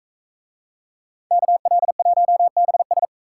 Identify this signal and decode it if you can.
Morse code, heard as a keyed tone, about 40 wpm, 700 Hz. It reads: KL1BS